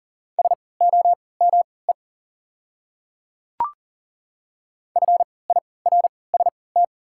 Morse code, heard as a keyed tone, about 38 wpm, spaced as for 22 wpm, 700 Hz. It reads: SOME FIRST
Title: 3x Character Spacing - Top 100 Words - 40wpm